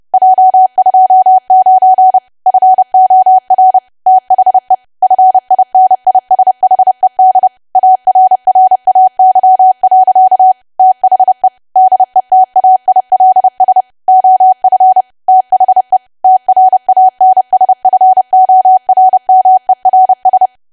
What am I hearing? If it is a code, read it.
Morse code, heard as a keyed tone, about 30 wpm, 750 Hz. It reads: J29 FOR THE FINISHED ARRAY. THE DETAILS OF THE TRANSFORMERS
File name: cwa / W1AW-Level3 / 30WPM